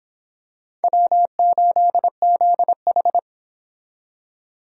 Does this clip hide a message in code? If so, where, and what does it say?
Yes: Morse code from 0.8 to 3.2 s, heard as a keyed tone, about 26 wpm, 700 Hz. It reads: W8ZH